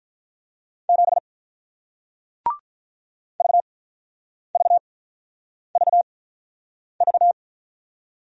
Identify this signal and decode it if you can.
Morse code, heard as a keyed tone, about 48 wpm, spaced as for 9 wpm, 700 Hz. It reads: 7 VVVV